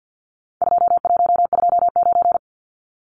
Morse code, heard as a keyed tone, about 50 wpm, 700 Hz. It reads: 2129
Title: Numbers - 4-Digit 50wpm